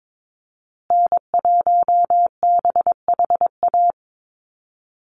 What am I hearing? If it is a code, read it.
Morse code, heard as a keyed tone, about 22 wpm, 700 Hz. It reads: N1BHA
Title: USA Call Signs Repeated 3x 22wpm